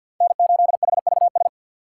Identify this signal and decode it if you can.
Morse code, heard as a keyed tone, about 50 wpm, 700 Hz. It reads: N8HVS
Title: USA Call Signs - 5 Characters - 6x Character Spacing - 50wpm